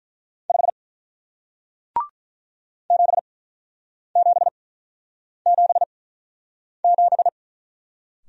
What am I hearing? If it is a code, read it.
Morse code, heard as a keyed tone, about 47 wpm, spaced as for 9 wpm, 700 Hz. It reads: 5 7777